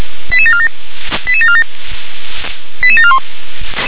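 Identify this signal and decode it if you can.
Five-tone selective call (ZVEI), heard as quick runs of five tones, tones of about 70 ms. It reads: 88747 88747 88741
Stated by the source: Received with an SDR; ZVEI five-tone selective call.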